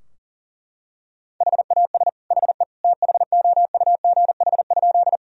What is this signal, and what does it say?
Morse code, heard as a keyed tone, about 37 wpm, 700 Hz. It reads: HAS HE THOUGH?
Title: Sentences from Top 1000 Words - Review Part 2 - 40wpm